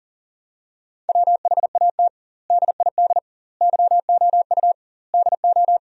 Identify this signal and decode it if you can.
Morse code, heard as a keyed tone, about 37 wpm, 700 Hz. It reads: WHAT DID YOU DO
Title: 2x Word-Spacing - Sentences from Top 100 Words - 40wpm